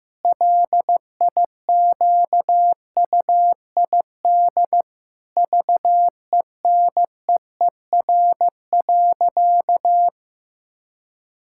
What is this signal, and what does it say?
Morse code, heard as a keyed tone, about 15 wpm, 700 Hz. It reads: LIQUID VENEER.
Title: Pangram Challenge v2 - 15wpm